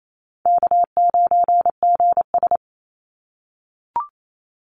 Morse code, heard as a keyed tone, about 28 wpm, 700 Hz. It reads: K9GS
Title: Call Signs - 28wpm